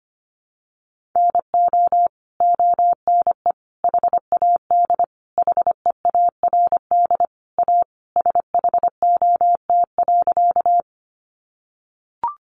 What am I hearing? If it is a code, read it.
Morse code, heard as a keyed tone, about 25 wpm, 700 Hz. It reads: NO ONE HAD HEARD A SHOT.
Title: Sentences from Top 1100 Words 25wpm